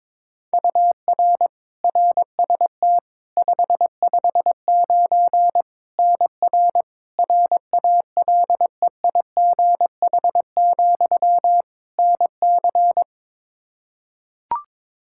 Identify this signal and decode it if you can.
Morse code, heard as a keyed tone, about 22 wpm, 700 Hz. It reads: UR RST 559 NR RALEIGH, NC
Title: Practice QSO - Exchange 1 - Beginning - 22wpm